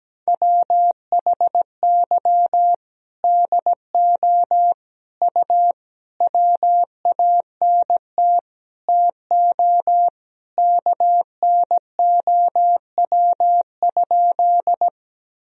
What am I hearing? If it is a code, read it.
Morse code, heard as a keyed tone, about 17 wpm, 700 Hz. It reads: WHY DO U WANT TO KNOW?
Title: Sentences from Top 200 Words and Abbreviations 17wpm